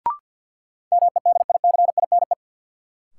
Morse code, heard as a keyed tone, about 53 wpm, 700 Hz. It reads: MEDICINE
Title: Sentences from Top 900 Words - New Word Drill - 50wpm